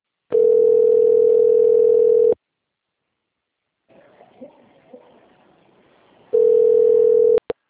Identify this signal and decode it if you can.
Ringback tone (North America), 440 + 480 Hz.
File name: rg-1600-unknown- - - .79469